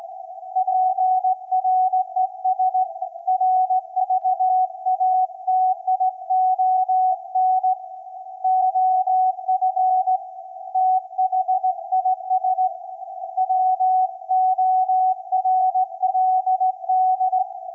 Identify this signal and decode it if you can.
Morse code, heard as a keyed tone, about 17 wpm, 750 Hz. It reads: PRESRVATION OF THIS WORLD
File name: iz7slz / VLF / SAQ 24-12-2024